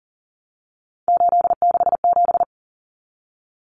Morse code, heard as a keyed tone, about 40 wpm, 700 Hz. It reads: 867
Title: Common Cut Numbers - 3-Digit 40wpm